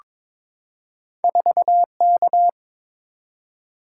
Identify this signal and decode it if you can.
Morse code, heard as a keyed tone, about 22 wpm, 700 Hz. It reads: 4K
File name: practice / 22wpm / Two Letter-Number Combinations - 22wpm